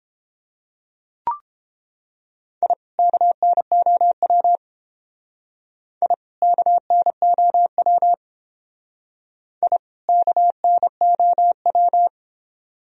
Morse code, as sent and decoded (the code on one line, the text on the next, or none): .. / -.- -. --- .-- / .. / -.- -. --- .-- / .. / -.- -. --- .--
I KNOW I KNOW I KNOW